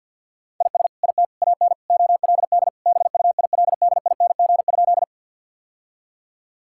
Morse code, heard as a keyed tone, about 48 wpm, 700 Hz. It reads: IS IT AN OLD BUILDING?
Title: Sentences from Top 1000 Words - Review Part 5 - 50wpm